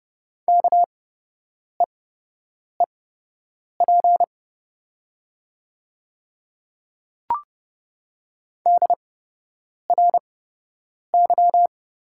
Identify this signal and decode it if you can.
Morse code, heard as a keyed tone, about 30 wpm, spaced as for 8 wpm, 700 Hz. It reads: KEEP DRY
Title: 8x Character Spacing - Top 500 Words - 30wpm